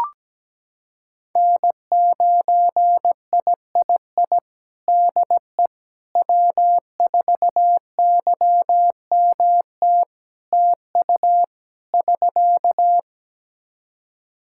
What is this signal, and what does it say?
Morse code, heard as a keyed tone, about 17 wpm, 700 Hz. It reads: N9III DE W4YMT TU <SK>
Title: Practice QSO - Exchange 4 - Ending - 17wpm